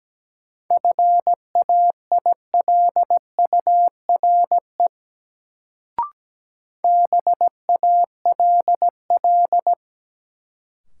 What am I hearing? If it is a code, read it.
Morse code, heard as a keyed tone, about 17 wpm, 700 Hz. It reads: FAILURE BALL
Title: Sentences from Top 900 Words - New Word Drill - 17wpm